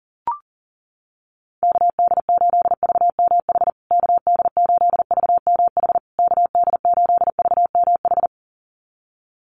Morse code, heard as a keyed tone, about 40 wpm, 700 Hz. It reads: KD8VMH KD8VMH KD8VMH
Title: USA Call Signs Repeated 3x 40wpm